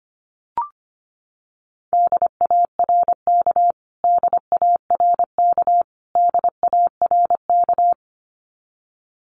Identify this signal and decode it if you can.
Morse code, heard as a keyed tone, about 25 wpm, 700 Hz. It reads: DARK DARK DARK
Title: Top 500 Words Repeated 25wpm